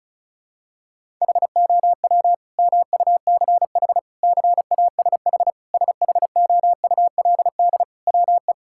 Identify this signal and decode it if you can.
Morse code, heard as a keyed tone, about 34 wpm, 700 Hz. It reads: HOW MUCH CASH SHOULD WE
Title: Sentences from Top 1000 Words - Review Part 5 - 35wpm